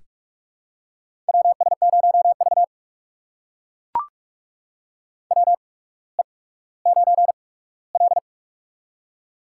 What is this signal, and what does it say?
Morse code, heard as a keyed tone, about 44 wpm, spaced as for 19 wpm, 690 Hz. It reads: WS0V WE9L
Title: USA Call Signs - 4 Characters - 8x Character Spacing - 45wpm